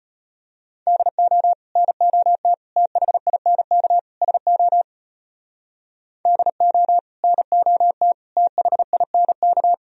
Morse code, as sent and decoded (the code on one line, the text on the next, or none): -.. --- / -. --- - / - .... .. -. -.- / ... --- / -.. --- / -. --- - / - .... .. -. -.-
DO NOT THINK SO DO NOT THINK